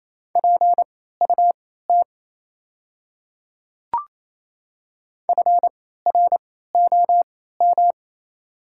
Morse code, heard as a keyed tone, about 28 wpm, spaced as for 16 wpm, 700 Hz. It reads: PUT FROM
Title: 3x Character Spacing - Top 500 Words - 28wpm